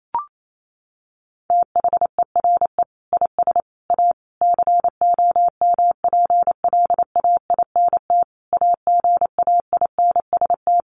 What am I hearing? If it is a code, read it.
Morse code, heard as a keyed tone, about 28 wpm, 700 Hz. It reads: THERE IS A COMPLAINT AGAINST
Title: Sentences from Top 1400 Words 28wpm